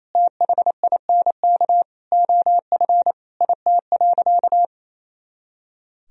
Morse code, heard as a keyed tone, about 28 wpm, 700 Hz. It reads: THINK OF IT.
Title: Sentences from Top 200 Words and Abbreviations 28wpm